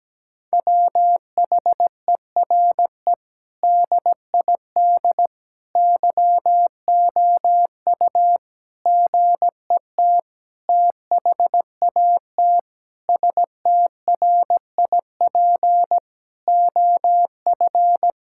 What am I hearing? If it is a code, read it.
Morse code, heard as a keyed tone, about 17 wpm, 700 Hz. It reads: WHERE DID YOU GET THAT STRIP OF
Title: Sentences from Top 1600 Words 17wpm